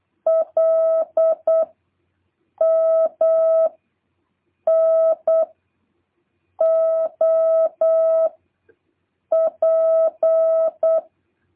Morse code, heard as a keyed tone, about 8 wpm, 650 Hz. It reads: L M N O P